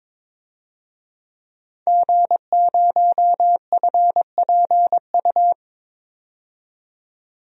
Morse code, heard as a keyed tone, about 22 wpm, 700 Hz. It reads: G0FPU
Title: DXCC-100 Call Signs - 5 Characters - 22wpm